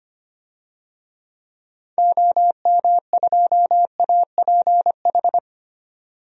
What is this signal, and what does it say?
Morse code, heard as a keyed tone, about 25 wpm, 700 Hz. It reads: OM2APH
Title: DXCC-100 Call Signs - 6 Characters - 25wpm